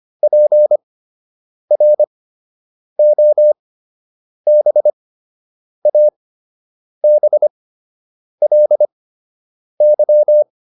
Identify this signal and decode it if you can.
Morse code, heard as a keyed tone, about 25 wpm, spaced as for 8 wpm, 600 Hz. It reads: PROBABLY